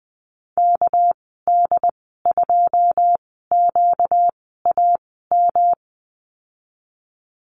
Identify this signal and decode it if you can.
Morse code, heard as a keyed tone, about 20 wpm, 700 Hz. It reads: K D 2 Q A M
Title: 238 Lesson 38g - Letter q - Call Signs 20wpm